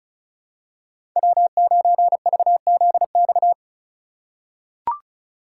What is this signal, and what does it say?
Morse code, heard as a keyed tone, about 36 wpm, 700 Hz. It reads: W9VZX
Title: USA Call Signs - 5 Characters - 3x Character Spacing - 35wpm